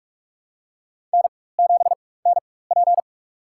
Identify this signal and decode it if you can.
Morse code, heard as a keyed tone, about 47 wpm, spaced as for 21 wpm, 700 Hz. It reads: N7NP
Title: USA Call Signs - 4 Characters - 4x Character Spacing - 45wpm